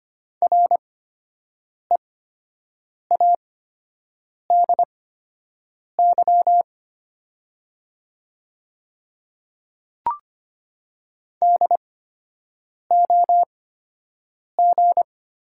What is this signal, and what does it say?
Morse code, heard as a keyed tone, about 25 wpm, spaced as for 7 wpm, 690 Hz. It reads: READY DOG
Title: 8x Character Spacing - Top 500 Words - 25wpm